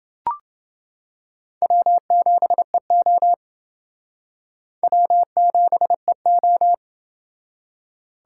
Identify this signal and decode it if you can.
Morse code, heard as a keyed tone, about 28 wpm, 700 Hz. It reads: W7EO W7EO